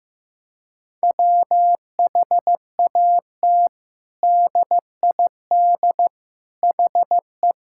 Morse code, heard as a keyed tone, about 15 wpm, 700 Hz. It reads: WHAT DID HE
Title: Sentences from Top 1000 Words - Review Part 6 - 15wpm